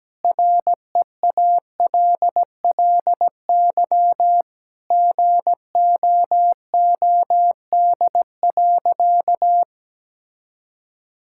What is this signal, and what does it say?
Morse code, heard as a keyed tone, about 17 wpm, 700 Hz. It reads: REALLY GOOD.
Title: Sentences from Top 1000 Words - Review Part 6 - 17wpm